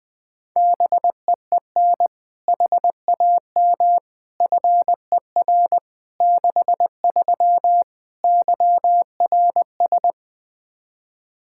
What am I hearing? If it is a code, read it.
Morse code, heard as a keyed tone, about 21 wpm, 700 Hz. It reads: BEEN HAM FER 63 YRS
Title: Practice QSO - Exchange 3 - More Info - 20wpm